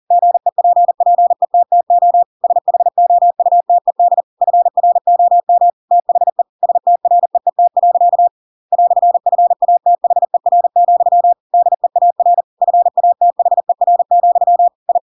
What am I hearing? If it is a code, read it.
Morse code, heard as a keyed tone, about 40 wpm, 700 Hz. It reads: GEPPETTO SHOUTED FROM THE STREET. "FATHER, DEAR FATHER, I
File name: Pinocchio / Pinocchio ch7 40 40 ik1hge